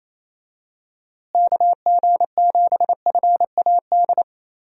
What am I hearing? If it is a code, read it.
Morse code, heard as a keyed tone, about 28 wpm, 700 Hz. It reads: KG7FAD
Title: USA Call Signs - 6 Characters - Repeated 2x - 28wpm